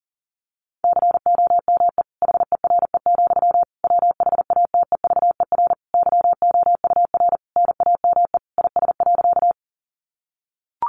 Morse code, heard as a keyed tone, about 40 wpm, 700 Hz. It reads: COME HERE, WHATEVER YOUR NAME IS.
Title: Sentences from Top 1100 Words 40wpm